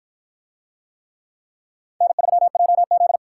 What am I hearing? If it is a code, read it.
Morse code, heard as a keyed tone, about 53 wpm, 700 Hz. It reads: N3JZ